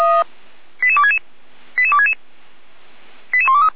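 Five-tone selective call (ZVEI), heard as quick runs of five tones, tones of about 70 ms. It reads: 88266 88266 88123
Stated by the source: ZVEI five-tone call received by SDR.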